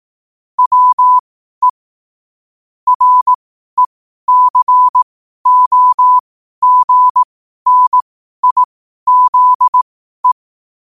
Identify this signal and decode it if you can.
Morse code, heard as a keyed tone, about 18 wpm, spaced as for 12 wpm, 1000 Hz. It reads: WE RECOGNIZE
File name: qotdfiles / 12wpm